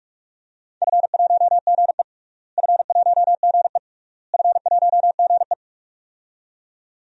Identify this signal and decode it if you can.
Morse code, heard as a keyed tone, about 46 wpm, 700 Hz. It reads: F1GE F1GE F1GE
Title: DXCC-100 Call Signs - 4 Characters - Repeated 3x - 45wpm